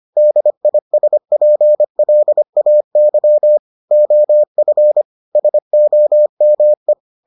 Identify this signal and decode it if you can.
Morse code, heard as a keyed tone, about 25 wpm, 600 Hz. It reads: DISPLAY OF SOME